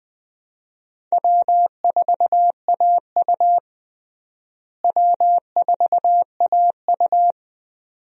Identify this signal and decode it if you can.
Morse code, heard as a keyed tone, about 20 wpm, 700 Hz. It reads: W4AU W4AU
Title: USA Call Signs - 4 Characters - Repeated 2x - 20wpm